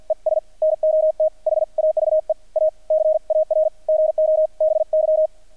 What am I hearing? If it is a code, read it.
Morse code, heard as a keyed tone, about 48 wpm, 630 Hz. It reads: ES NOT HAVE A CAR GO BY
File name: archived / CRSnet